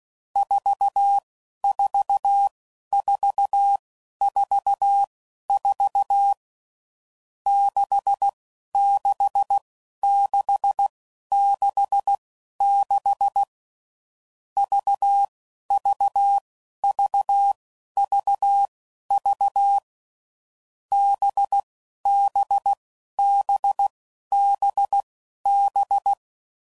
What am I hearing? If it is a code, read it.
Morse code, heard as a keyed tone, about 16 wpm, spaced as for 11 wpm, 800 Hz. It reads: 44444 66666 VVVVV BBBBB
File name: Groups / Basic Letter Groups / VB46